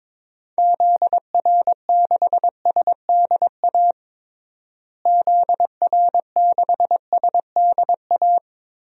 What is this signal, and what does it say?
Morse code, heard as a keyed tone, about 22 wpm, 700 Hz. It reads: ZR6SDA ZR6SDA
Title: DXCC-100 Call Signs - 6 Characters - Repeated 3x - 22wpm